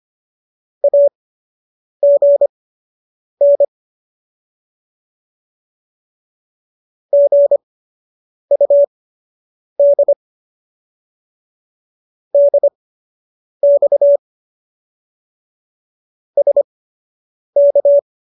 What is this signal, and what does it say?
Morse code, heard as a keyed tone, about 25 wpm, spaced as for 8 wpm, 570 Hz. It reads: AGN GUD DX SK